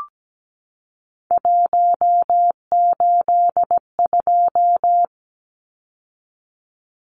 Morse code, heard as a keyed tone, about 17 wpm, 700 Hz. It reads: 182
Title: Numbers - 3-Digit 17wpm